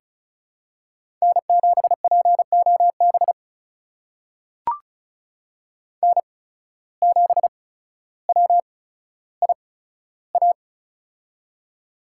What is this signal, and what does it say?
Morse code, heard as a keyed tone, about 36 wpm, spaced as for 15 wpm, 700 Hz. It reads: N7POB N7WIA
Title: USA Call Signs - 5 Characters - 8x Character Spacing - 35wpm